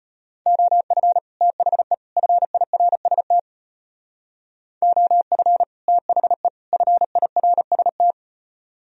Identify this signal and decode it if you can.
Morse code, heard as a keyed tone, about 30 wpm, 700 Hz. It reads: OF THE FIRST OF THE FIRST
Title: Speed-Racing - Sets of 3 Words 25wpm